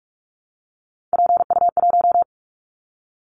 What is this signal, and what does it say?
Morse code, heard as a keyed tone, about 44 wpm, 700 Hz. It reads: PU1
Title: Callsign Prefixes for Top 100 DXCC - 45wpm